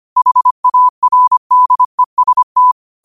Morse code, heard as a keyed tone, about 25 wpm, 1000 Hz. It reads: HARDEST
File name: qotdfiles / 25wpm